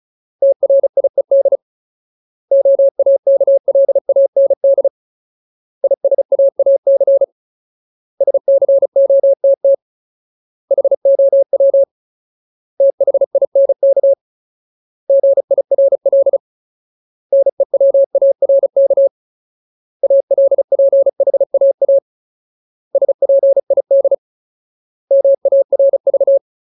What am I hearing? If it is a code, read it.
Morse code, heard as a keyed tone, about 34 wpm, 550 Hz. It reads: TRIED OAKLAND ISAAC SCOTT HOW THINK GIRL NEWARK ALPHAA SPID MARV
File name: Ultimate Word List / 35WPM Ultimate